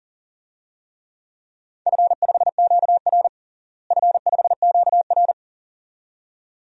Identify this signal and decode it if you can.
Morse code, heard as a keyed tone, about 37 wpm, 700 Hz. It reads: F5QR F5QR
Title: DXCC-100 Call Signs - 4 Characters - Repeated 2x - 40wpm